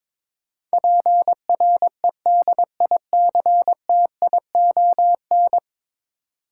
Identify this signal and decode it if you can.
Morse code, heard as a keyed tone, about 22 wpm, 700 Hz. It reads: PREDICTION
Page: morse-code-ninja-practice-22wpm